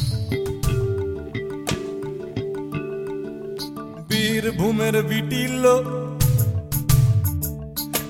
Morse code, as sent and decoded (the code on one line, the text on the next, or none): none